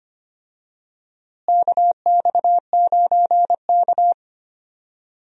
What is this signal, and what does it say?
Morse code, heard as a keyed tone, about 25 wpm, 700 Hz. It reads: KX9K
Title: USA Call Signs - 4 Characters - Repeated 2x - 25wpm